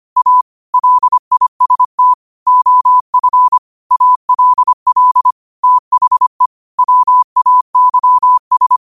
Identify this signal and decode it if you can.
Morse code, heard as a keyed tone, about 25 wpm, 1000 Hz. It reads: A LIST OF ALL THE WAYS
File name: qotdfiles / 25wpm